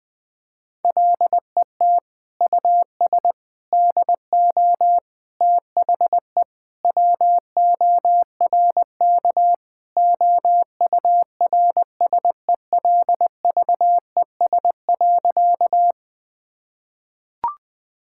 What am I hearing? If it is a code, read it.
Morse code, heard as a keyed tone, about 21 wpm, 700 Hz. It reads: LET US DO THE WORK OURSELVES.
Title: Sentences from Top 1600 Words 20wpm